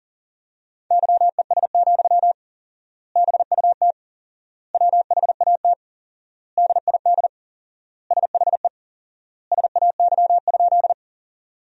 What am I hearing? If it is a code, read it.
Morse code, heard as a keyed tone, about 37 wpm, 700 Hz. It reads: YES, BUT WHAT DID SHE SAY?
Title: 4x Word-Spacing - Sentences from Top 500 Words - 40wpm